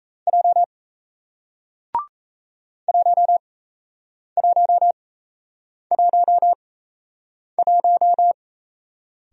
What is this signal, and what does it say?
Morse code, heard as a keyed tone, about 38 wpm, spaced as for 8 wpm, 700 Hz. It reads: J 1111